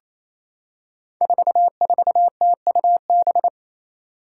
Morse code, heard as a keyed tone, about 27 wpm, 700 Hz. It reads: 44TUB
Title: Five Challenging Letter-Number Combinations 28wpm